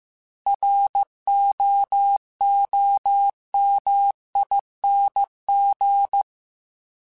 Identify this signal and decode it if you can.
Morse code, heard as a keyed tone, about 15 wpm, 800 Hz. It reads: ROOMING